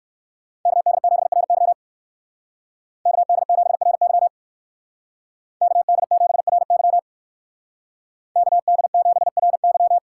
Morse code, heard as a keyed tone, about 58 wpm, 700 Hz. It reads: KD7RY KD7RY KD7RY KD7RY